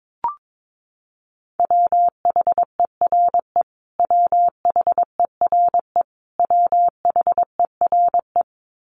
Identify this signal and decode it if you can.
Morse code, heard as a keyed tone, about 22 wpm, 700 Hz. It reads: WHERE WHERE WHERE
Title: Top 500 Words Repeated 22wpm